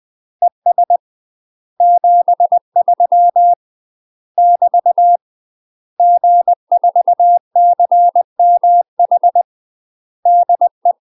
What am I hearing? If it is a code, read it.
Morse code, heard as a keyed tone, about 20 wpm, 700 Hz. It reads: ES 73 <BT> G4CMH DE